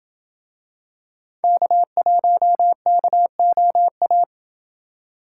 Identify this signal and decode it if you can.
Morse code, heard as a keyed tone, about 27 wpm, 700 Hz. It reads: K1KOA